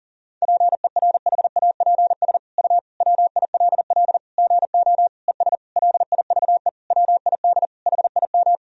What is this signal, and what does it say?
Morse code, heard as a keyed tone, about 40 wpm, 700 Hz. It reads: PERHAPS U WILL GO ES LIVE WID HIM
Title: Sentences from Top 500 Words and Abbreviations 40wpm